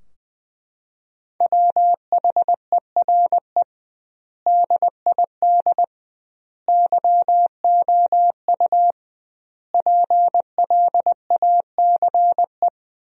Morse code, heard as a keyed tone, about 20 wpm, 700 Hz. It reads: WHERE DID YOU PLACE
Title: 2x Word-Spacing - Sentences from Top 200 Words - 20wpm